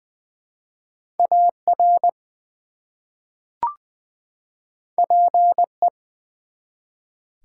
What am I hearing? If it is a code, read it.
Morse code, heard as a keyed tone, about 20 wpm, 700 Hz. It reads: AR PE
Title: ARRL Sections v2 - 20wpm